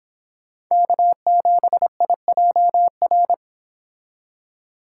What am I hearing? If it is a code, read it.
Morse code, heard as a keyed tone, about 26 wpm, 700 Hz. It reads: K7IJR